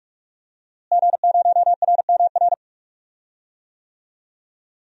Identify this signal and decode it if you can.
Morse code, heard as a keyed tone, about 41 wpm, 700 Hz. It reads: G0RMR
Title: DXCC-100 Call Signs - 5 Characters - Repeated 3x - 45wpm